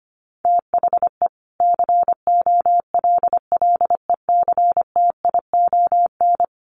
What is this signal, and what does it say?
Morse code, heard as a keyed tone, about 25 wpm, 700 Hz. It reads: THE COLLECTION
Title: Sentences from Top 1200 Words 25wpm